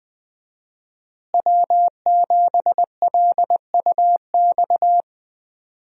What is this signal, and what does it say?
Morse code, heard as a keyed tone, about 20 wpm, 700 Hz. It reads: W7LUX
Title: USA Call Signs - 5 Characters - Repeated 3x - 20wpm